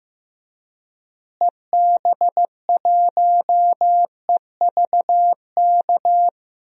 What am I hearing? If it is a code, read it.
Morse code, heard as a keyed tone, about 15 wpm, 700 Hz. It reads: EB1EVK
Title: DXCC-100 Call Signs - 6 Characters - Repeated 2x - 15wpm